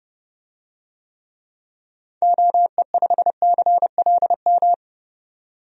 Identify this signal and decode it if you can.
Morse code, heard as a keyed tone, about 30 wpm, 700 Hz. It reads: OE5CLM
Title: DXCC-100 Call Signs - 6 Characters - 30wpm